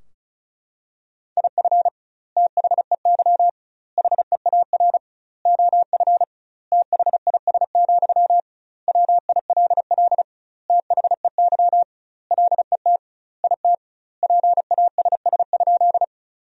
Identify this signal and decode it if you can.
Morse code, heard as a keyed tone, about 34 wpm, 700 Hz. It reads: IF THEY HEAR OF THIS, WILL THEY LET IT PASS?
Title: 2x Word-Spacing - Sentences from Top 400 Words - 35wpm